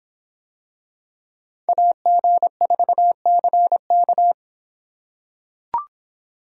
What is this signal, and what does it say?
Morse code, heard as a keyed tone, about 24 wpm, 700 Hz. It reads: AG4CK